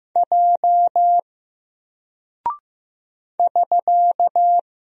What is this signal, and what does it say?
Morse code, heard as a keyed tone, about 15 wpm, 700 Hz. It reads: J <SK>
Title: Single Character (Letter:Number:Punctuation:Prosign) Alt 15wpm